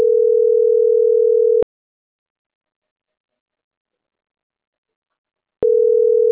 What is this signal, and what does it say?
Ringback tone (North America), 440 + 480 Hz.